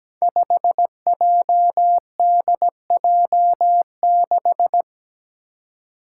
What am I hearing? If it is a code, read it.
Morse code, heard as a keyed tone, about 17 wpm, 700 Hz. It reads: 5JDJ6
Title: Five Challenging Letter-Number Combinations 17wpm